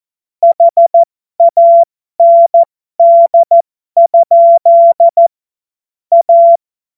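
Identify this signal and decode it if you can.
Morse code, heard as a keyed tone, about 14 wpm, spaced as for 12 wpm, 680 Hz. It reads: HAND? A
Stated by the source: Amateur radio CW practice text made of jokes.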